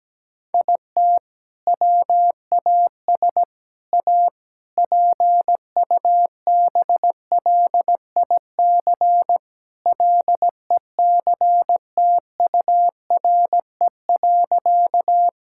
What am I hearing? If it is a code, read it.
Morse code, heard as a keyed tone, about 17 wpm, 700 Hz. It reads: IT WAS A PUBLIC LECTURE.
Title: Sentences from Top 1600 Words 17wpm